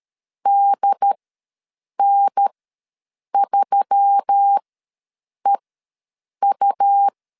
Morse code, heard as a keyed tone, about 13 wpm, spaced as for 7 wpm, 800 Hz. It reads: DN3EU
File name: LEZIONI / 5 LBVC / 5 L ws 12wpm 2958